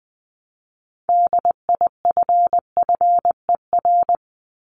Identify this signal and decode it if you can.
Morse code, heard as a keyed tone, about 20 wpm, 700 Hz. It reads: DIFFER
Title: Top 500 Words 20wpm